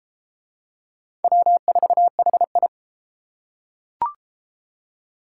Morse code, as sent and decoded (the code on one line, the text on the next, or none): .-- ....- .... ..
W4HI